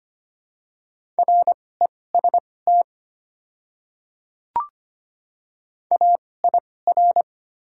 Morse code, heard as a keyed tone, about 25 wpm, spaced as for 18 wpm, 700 Hz. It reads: REST AIR
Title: 2x Character Spacing - Top 500 Words - 25wpm